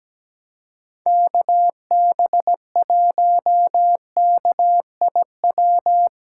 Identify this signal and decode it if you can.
Morse code, heard as a keyed tone, about 17 wpm, 700 Hz. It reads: KB1KIW
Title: USA Callsign DE Callsign 17wpm